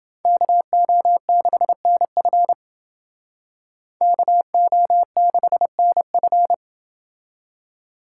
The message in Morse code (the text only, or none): KO6NF KO6NF